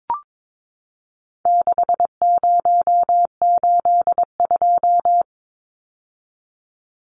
Morse code, heard as a keyed tone, about 22 wpm, 700 Hz. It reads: 6082
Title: Numbers - 4-Digit 22wpm